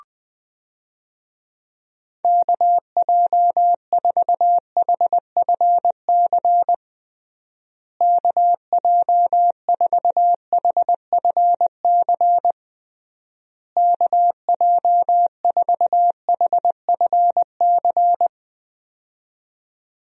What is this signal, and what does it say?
Morse code, heard as a keyed tone, about 21 wpm, 700 Hz. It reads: KJ4HFC KJ4HFC KJ4HFC
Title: USA Call Signs - 6 Characters - Repeated 3x - 20wpm